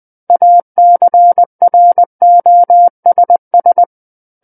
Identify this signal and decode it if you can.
Morse code, heard as a keyed tone, about 20 wpm, 700 Hz. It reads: ACROSS
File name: 20 WPM